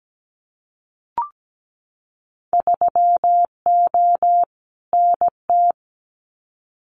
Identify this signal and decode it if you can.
Morse code, heard as a keyed tone, about 17 wpm, 700 Hz. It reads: 3O NT
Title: Winter Field Day Exchanges Encore 17wpm